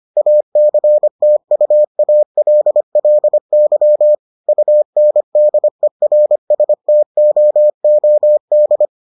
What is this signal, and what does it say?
Morse code, heard as a keyed tone, about 25 wpm, 600 Hz. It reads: ACTUALLY UNDERSTOOD